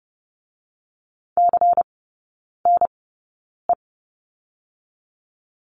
Morse code, heard as a keyed tone, about 30 wpm, spaced as for 9 wpm, 700 Hz. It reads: CNE
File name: Lesson 16e - Letter c - Three Characters 30wpm